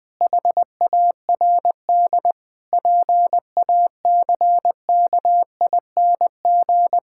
Morse code, heard as a keyed tone, about 20 wpm, 700 Hz. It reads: HARD PACKING
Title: Pangram Challenge v2 - 20wpm